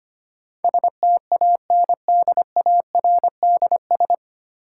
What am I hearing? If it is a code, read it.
Morse code, heard as a keyed tone, about 25 wpm, 700 Hz. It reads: STANDARDS
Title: Sentences from Top 1000 Words - Review Part 6 - New Word Drill - 25wpm